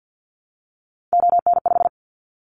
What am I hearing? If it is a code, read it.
Morse code, heard as a keyed tone, about 50 wpm, 700 Hz. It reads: ON5
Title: Callsign Prefixes for Top 25 DXCC - 50wpm